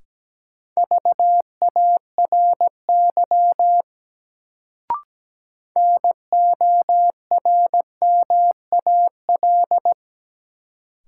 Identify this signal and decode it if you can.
Morse code, heard as a keyed tone, about 17 wpm, 700 Hz. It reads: VARY NORMAL
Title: Sentences from Top 1200 Words - New Word Drill Alt - 17wpm